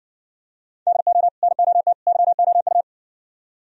Morse code, heard as a keyed tone, about 69 wpm, 700 Hz. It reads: DO NOT YOU